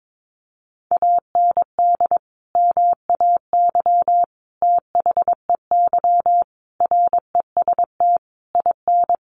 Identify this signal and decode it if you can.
Morse code, heard as a keyed tone, about 22 wpm, 700 Hz. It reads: AND MAY THEY REST IN
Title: Sentences from Top 1500 Words 22wpm